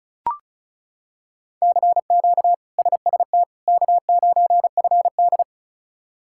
Morse code, heard as a keyed tone, about 36 wpm, 700 Hz. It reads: CQ SST K9FD
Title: K1USN SST QSO Practice - 35wpm